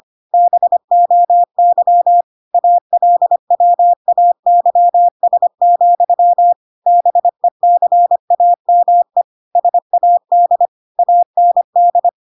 Morse code, heard as a keyed tone, about 25 wpm, 700 Hz. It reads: BOY ALWAYS, BECAME SAD AND